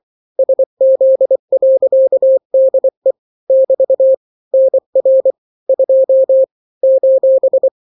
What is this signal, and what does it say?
Morse code, heard as a keyed tone, about 24 wpm, 530 Hz. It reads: SZ.DE <BT> NR 2 :
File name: daily / 23wpm